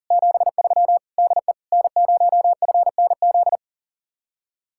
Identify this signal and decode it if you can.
Morse code, heard as a keyed tone, about 47 wpm, 700 Hz. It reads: 73 DE N0FNZ
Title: Speed-Racing Common QSO Phrases - 40wpm